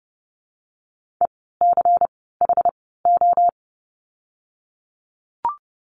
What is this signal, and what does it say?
Morse code, heard as a keyed tone, about 30 wpm, 700 Hz. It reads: E C H O
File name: Lesson 16f - Letter c - Words 30wpm